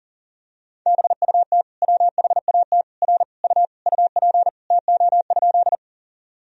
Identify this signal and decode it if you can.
Morse code, heard as a keyed tone, about 37 wpm, 700 Hz. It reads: BUT WHAT R U UP TO?
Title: Sentences from Top 100 Words and Abbreviations 40wpm